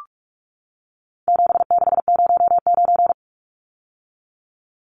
Morse code, heard as a keyed tone, about 44 wpm, 700 Hz. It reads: 7609
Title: Numbers - 4-Digit 45wpm